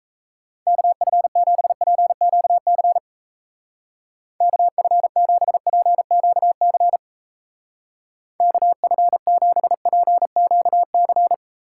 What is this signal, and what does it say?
Morse code, heard as a keyed tone, about 37 wpm, 700 Hz. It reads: KF7PQC KF7PQC KF7PQC